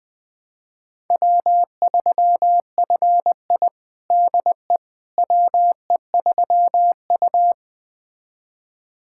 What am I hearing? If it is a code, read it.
Morse code, heard as a keyed tone, about 20 wpm, 700 Hz. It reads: W3FI DE WE3U
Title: USA Callsign DE Callsign - 4 Characters - 20wpm